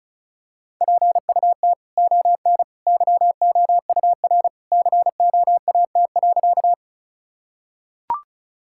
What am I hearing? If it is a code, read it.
Morse code, heard as a keyed tone, about 34 wpm, 700 Hz. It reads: PUT ON YOUR COAT.
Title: Sentences from Top 1600 Words 35wpm